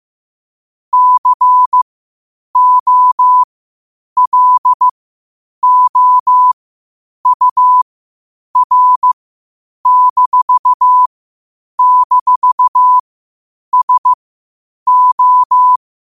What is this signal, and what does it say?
Morse code, heard as a keyed tone, about 15 wpm, spaced as for 8 wpm, 1000 Hz. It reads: COLOUR--SO
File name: qotdfiles / 8wpm